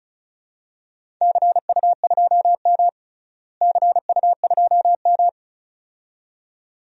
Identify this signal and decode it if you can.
Morse code, heard as a keyed tone, about 36 wpm, 700 Hz. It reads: CU2M CU2M
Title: DXCC-100 Call Signs - 4 Characters - Repeated 2x - 35wpm